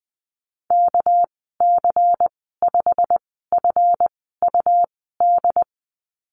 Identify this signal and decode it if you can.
Morse code, heard as a keyed tone, about 20 wpm, spaced as for 14 wpm, 700 Hz. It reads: KC5FUD
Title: 238 Lesson 38g - Letter q - Call Signs 20wpm